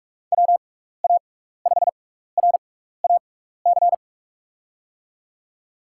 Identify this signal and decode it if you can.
Morse code, heard as a keyed tone, about 44 wpm, spaced as for 15 wpm, 700 Hz. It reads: WA5RAC
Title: USA Call Signs - 6 Characters - 6x Character Spacing - 45wpm